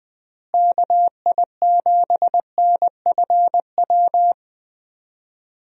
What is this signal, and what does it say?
Morse code, heard as a keyed tone, about 20 wpm, 700 Hz. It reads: KI7NFW
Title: USA Call Signs - 6 Characters - Repeated 2x - 20wpm